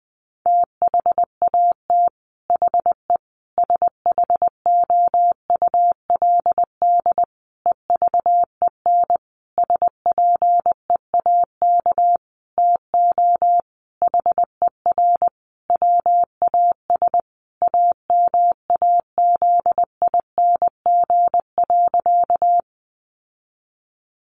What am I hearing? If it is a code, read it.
Morse code, heard as a keyed tone, about 20 wpm, 700 Hz. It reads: THAT HE SHOULD EVEN SPEAK TO HER WAS AMAZING.
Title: Sentences from Top 1500 Words 20wpm